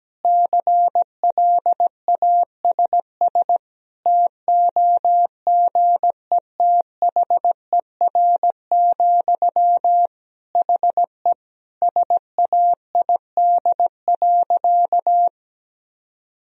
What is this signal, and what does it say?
Morse code, heard as a keyed tone, about 17 wpm, 700 Hz. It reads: CLASS TOGETHER, HE SAID.
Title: Sentences from Top 1000 Words - Review Part 2 - 17wpm